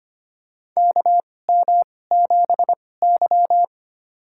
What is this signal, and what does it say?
Morse code, heard as a keyed tone, about 25 wpm, spaced as for 18 wpm, 700 Hz. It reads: KM7Y
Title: USA Call Signs - 4 Characters - 2x Character Spacing - 25wpm